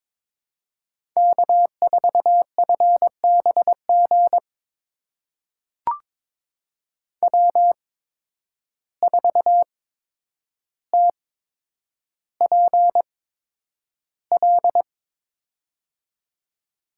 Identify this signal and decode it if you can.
Morse code, heard as a keyed tone, about 22 wpm, spaced as for 9 wpm, 700 Hz. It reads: K4FBG W4TPL
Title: USA Call Signs - 8x Character Spacing - 22wpm